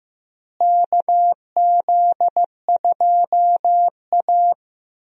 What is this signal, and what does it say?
Morse code, heard as a keyed tone, about 15 wpm, 700 Hz. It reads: KZ2A
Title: USA Call Signs - 4 Characters - Repeated 3x - 15wpm